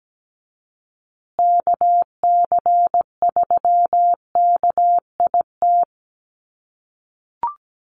Morse code, heard as a keyed tone, about 17 wpm, 700 Hz. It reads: KC3KIT
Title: USA Call Signs Repeated 3x 17wpm